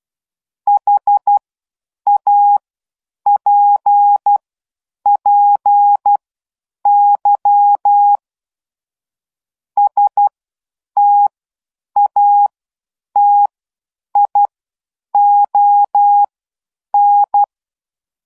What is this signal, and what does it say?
Morse code, heard as a keyed tone, about 12 wpm, spaced as for 8 wpm, 820 Hz. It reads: HAPPY STATION